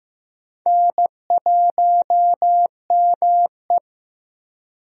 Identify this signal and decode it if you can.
Morse code, heard as a keyed tone, about 15 wpm, 700 Hz. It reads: N1ME
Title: USA Call Signs - 4 Characters - Repeated 3x - 15wpm